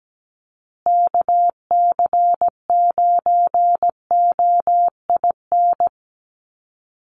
Morse code, heard as a keyed tone, about 17 wpm, 700 Hz. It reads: KC9OIN
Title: USA Call Signs Repeated 2x 17wpm